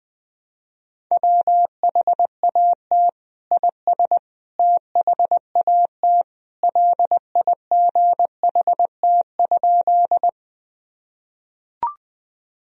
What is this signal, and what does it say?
Morse code, heard as a keyed tone, about 20 wpm, 700 Hz. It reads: WHAT IS THAT LIGHT?
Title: Sentences from Top 1000 Words - Review Part 1 - 20wpm